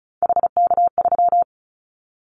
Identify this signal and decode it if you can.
Morse code, heard as a keyed tone, about 34 wpm, 700 Hz. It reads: HK3
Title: Callsign Prefixes for Top 100 DXCC - 35wpm